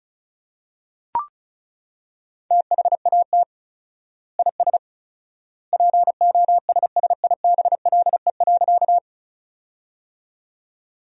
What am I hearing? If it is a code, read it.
Morse code, heard as a keyed tone, about 34 wpm, 700 Hz. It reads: THAT IS POSSIBLE.
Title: 4x Word-Spacing - Sentences from Top 500 Words - 35wpm